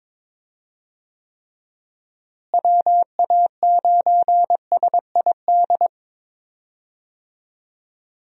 Morse code, heard as a keyed tone, about 22 wpm, 700 Hz. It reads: WA9SID
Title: USA Call Signs - 6 Characters - Repeated 2x - 22wpm